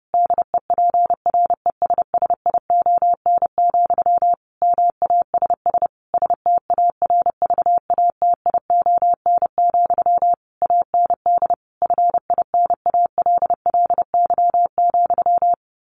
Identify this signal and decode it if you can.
Morse code, heard as a keyed tone, about 30 wpm, 700 Hz. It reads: DEPRESSION, MASS STARVATION, AND FINALLY,
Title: A Sound Of Thunder 30wpm